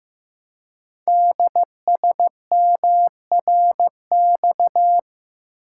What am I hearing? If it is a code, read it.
Morse code, heard as a keyed tone, about 15 wpm, 700 Hz. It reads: DSMRX
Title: Code Groups - 5 Letters - 15wpm